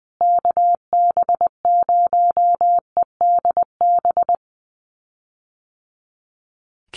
Morse code, heard as a keyed tone, about 20 wpm, 700 Hz. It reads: KB0EDB
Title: USA Call Signs Repeated 3x 20wpm